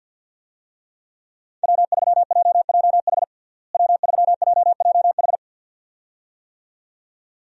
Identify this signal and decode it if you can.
Morse code, heard as a keyed tone, about 50 wpm, 700 Hz. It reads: W3JJH W3JJH
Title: USA Call Signs - 5 Characters - Repeated 2x - 50wpm